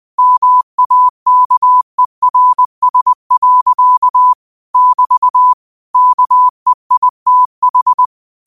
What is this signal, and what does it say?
Morse code, heard as a keyed tone, about 20 wpm, 1000 Hz. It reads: MAKERS. <BT> KEITH
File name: qotdfiles / 20wpm